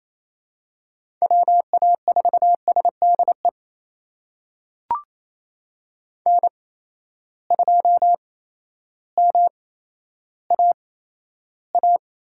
Morse code, heard as a keyed tone, about 28 wpm, spaced as for 13 wpm, 700 Hz. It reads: WA4SDE N2MAA
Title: USA Call Signs - 8x Character Spacing - 28wpm